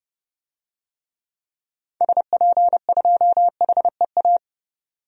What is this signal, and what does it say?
Morse code, heard as a keyed tone, about 31 wpm, 700 Hz. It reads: SP2HEA
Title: DXCC-100 Call Signs - 6 Characters - 30wpm